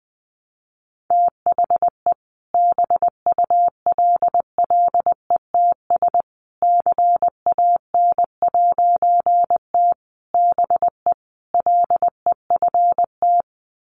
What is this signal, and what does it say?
Morse code, heard as a keyed tone, about 20 wpm, 700 Hz. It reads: THE BULLETS CAN'T BE LEFT
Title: A Sound Of Thunder 20wpm